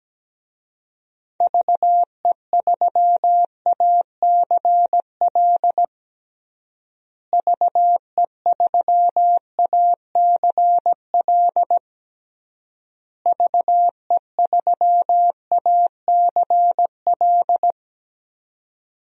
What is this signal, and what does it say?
Morse code, heard as a keyed tone, about 17 wpm, 700 Hz. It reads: VE3ACL VE3ACL VE3ACL
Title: DXCC-100 Call Signs - 6 Characters - Repeated 3x - 17wpm